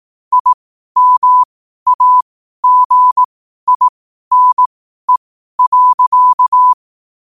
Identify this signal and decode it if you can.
Morse code, heard as a keyed tone, about 18 wpm, spaced as for 12 wpm, 1000 Hz. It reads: IMAGINE.
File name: qotdfiles / 12wpm